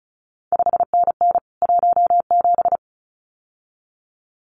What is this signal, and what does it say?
Morse code, heard as a keyed tone, about 36 wpm, 700 Hz. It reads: 5NN 17
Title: CQ WW DX Contest Exchange Encore - 35wpm